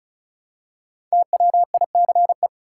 Morse code, heard as a keyed tone, about 34 wpm, 700 Hz. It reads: TWICE
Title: Five Letter Words - 35wpm